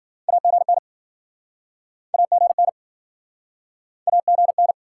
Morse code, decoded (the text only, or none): AGN AGN AGN